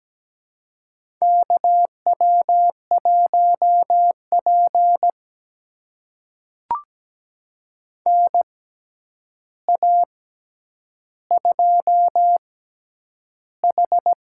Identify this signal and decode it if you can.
Morse code, heard as a keyed tone, about 17 wpm, spaced as for 9 wpm, 700 Hz. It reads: KW1P NA2H
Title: USA Call Signs - 6x Character Spacing - 17wpm